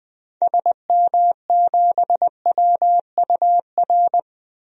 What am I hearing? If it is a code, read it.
Morse code, heard as a keyed tone, about 20 wpm, 700 Hz. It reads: SM7WUR
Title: DXCC-100 Call Signs - 6 Characters - 20wpm